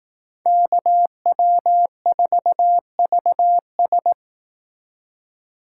Morse code, heard as a keyed tone, about 18 wpm, 700 Hz. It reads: KW4VS
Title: Speed Racing - USA 2x2 Call Signs 15wpm